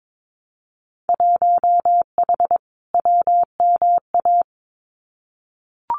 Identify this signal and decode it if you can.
Morse code, heard as a keyed tone, about 22 wpm, 700 Hz. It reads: 1H WMA
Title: Winter Field Day Exchanges 22wpm